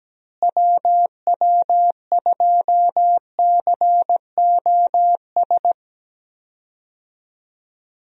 Morse code, heard as a keyed tone, about 17 wpm, 700 Hz. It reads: WW2COS
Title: USA Call Signs - 6 Characters - Repeated 3x - 17wpm